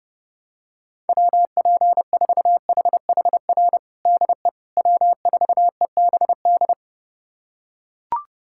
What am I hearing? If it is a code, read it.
Morse code, heard as a keyed tone, about 30 wpm, 700 Hz. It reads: WP4HHR DE W4EBD
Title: USA Callsign DE Callsign 30wpm